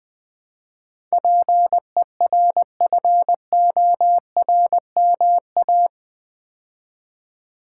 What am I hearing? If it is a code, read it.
Morse code, heard as a keyed tone, about 20 wpm, 700 Hz. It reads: PERFORMA
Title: Build A Word - 4 Letters At A Time - 11-Letter Words - 20wpm